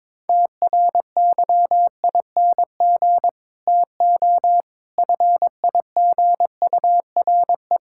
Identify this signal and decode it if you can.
Morse code, heard as a keyed tone, about 22 wpm, 700 Hz. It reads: TRYING TO FIGURE
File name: practice / 22wpm / Build A Phrase - 5-Word Phrases - 22wpm